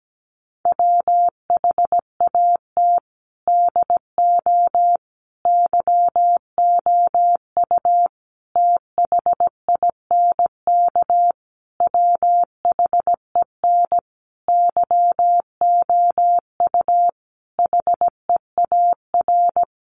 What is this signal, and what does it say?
Morse code, heard as a keyed tone, about 17 wpm, 700 Hz. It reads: WHAT DO YOU THINK WHEN YOU HEAR
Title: Sentences from Top 300 Words 17wpm